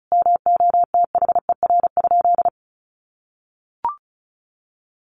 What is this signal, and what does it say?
Morse code, heard as a keyed tone, about 34 wpm, 700 Hz. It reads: MOTHER?
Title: Sentences from Top 1300 Words 35wpm